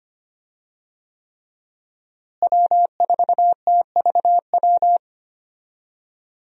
Morse code, heard as a keyed tone, about 25 wpm, 700 Hz. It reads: W4TVW
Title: USA Call Signs - 5 Characters - Repeated 3x - 25wpm